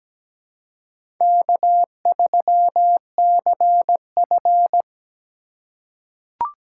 Morse code, heard as a keyed tone, about 17 wpm, 700 Hz. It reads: K3CF
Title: USA Call Signs - 4 Characters - 8x Character Spacing - 17wpm